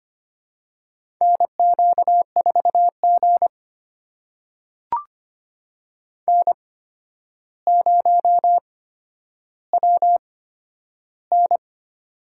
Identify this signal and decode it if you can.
Morse code, heard as a keyed tone, about 25 wpm, spaced as for 11 wpm, 700 Hz. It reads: NQ4G N0WN
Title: USA Call Signs - 4 Characters - 8x Character Spacing - 25wpm